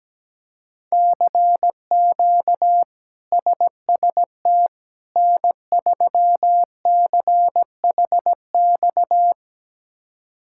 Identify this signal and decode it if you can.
Morse code, heard as a keyed tone, about 17 wpm, 700 Hz. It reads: CQ SST N3CHX
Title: K1USN SST QSO Practice - 17wpm